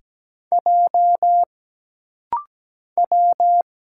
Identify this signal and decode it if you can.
Morse code, heard as a keyed tone, about 17 wpm, 700 Hz. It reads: J W
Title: Single Letter-Number Alt - Rapid-Fire 17wpm